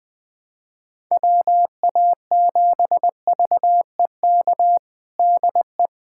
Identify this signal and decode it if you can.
Morse code, heard as a keyed tone, about 21 wpm, 700 Hz. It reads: WA7VEK DE
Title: USA Callsign DE Callsign - 6 Characters - 20wpm